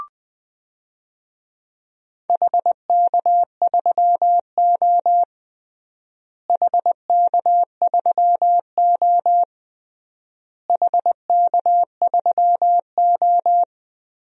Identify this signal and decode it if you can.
Morse code, heard as a keyed tone, about 21 wpm, 700 Hz. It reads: HK3O HK3O HK3O
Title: DXCC-100 Call Signs - 4 Characters - Repeated 3x - 20wpm